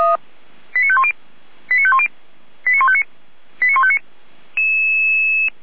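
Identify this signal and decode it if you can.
Five-tone selective call (ZVEI), heard as quick runs of five tones, tones of about 70 ms. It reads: 89510 89510 89159 89159
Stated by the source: ZVEI five-tone selective call received on SDR.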